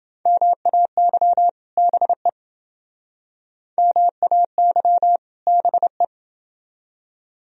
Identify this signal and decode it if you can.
Morse code, heard as a keyed tone, about 28 wpm, 700 Hz. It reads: MAY BE MAY BE